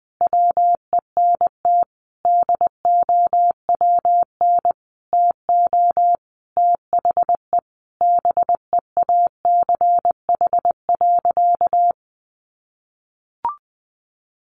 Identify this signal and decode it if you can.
Morse code, heard as a keyed tone, about 20 wpm, 700 Hz. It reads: WENT DOWN TO THE BEACH.
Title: Sentences from Top 1500 Words 20wpm